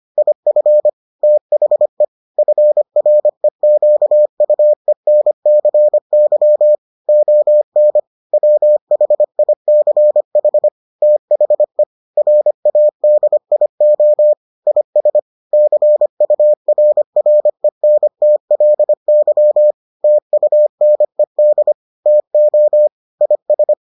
Morse code, heard as a keyed tone, about 25 wpm, 600 Hz. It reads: IF THE FREQUENCY ON WHICH THE RADIO IS CURRENTLY TUNED TO IS